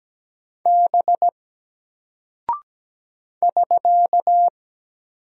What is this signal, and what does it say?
Morse code, heard as a keyed tone, about 17 wpm, 700 Hz. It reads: B <SK>
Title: Single Character (Letter:Number:Punctuation:Prosign) Alt 17wpm